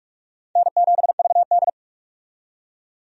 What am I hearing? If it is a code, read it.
Morse code, heard as a keyed tone, about 46 wpm, 700 Hz. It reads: N7VD
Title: USA Call Signs - 4 Characters - 45wpm